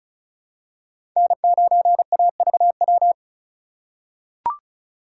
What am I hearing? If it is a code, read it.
Morse code, heard as a keyed tone, about 36 wpm, 700 Hz. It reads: N9AVW
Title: USA Call Signs - 6x Character Spacing - 35wpm